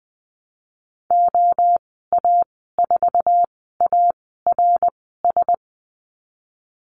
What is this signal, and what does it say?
Morse code, heard as a keyed tone, about 19 wpm, spaced as for 14 wpm, 700 Hz. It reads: OA4ARS
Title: 076 Lesson 13g - Letter L - Call Signs 20wpm